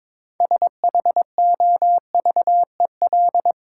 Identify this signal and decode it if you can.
Morse code, heard as a keyed tone, about 24 wpm, 690 Hz. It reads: SHOVEL
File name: Easy Words - Speed-Racing with Hints - 22wpm